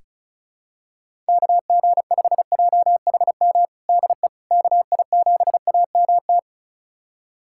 Morse code, heard as a keyed tone, about 36 wpm, 700 Hz. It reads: KG5JHM DE KI7AMT
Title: USA Callsign DE Callsign - 6 Characters - 35wpm